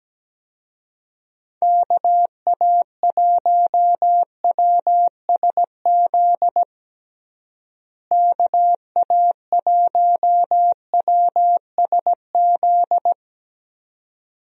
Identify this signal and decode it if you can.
Morse code, heard as a keyed tone, about 17 wpm, 700 Hz. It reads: KA1WSZ KA1WSZ
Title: USA Call Signs - 6 Characters - Repeated 2x - 17wpm